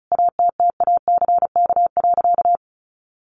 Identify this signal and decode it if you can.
Morse code, heard as a keyed tone, about 34 wpm, 700 Hz. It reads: ATTACK.
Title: Sentences from Top 1500 Words 35wpm